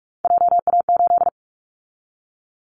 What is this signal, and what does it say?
Morse code, heard as a keyed tone, about 46 wpm, 700 Hz. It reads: JA8
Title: Callsign Prefixes for Top 25 DXCC - 45wpm